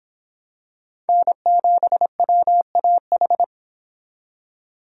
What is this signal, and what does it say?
Morse code, heard as a keyed tone, about 26 wpm, 700 Hz. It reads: N7WAH